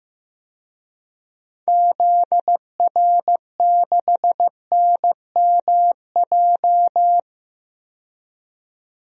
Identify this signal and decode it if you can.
Morse code, heard as a keyed tone, about 15 wpm, 700 Hz. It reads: ZR6NMJ
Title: DXCC-100 Call Signs - 6 Characters - 15wpm